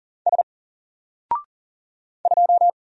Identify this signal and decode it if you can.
Morse code, heard as a keyed tone, about 39 wpm, 700 Hz. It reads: S 2
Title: Single Letter-Number Alt 40wpm